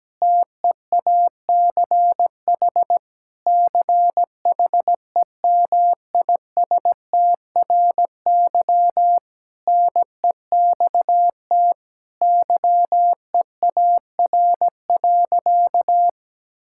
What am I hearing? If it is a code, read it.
Morse code, heard as a keyed tone, about 17 wpm, 700 Hz. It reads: TEACH CHEMISTRY NEXT YEAR.
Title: Sentences from Top 1000 Words - Review Part 6 - 17wpm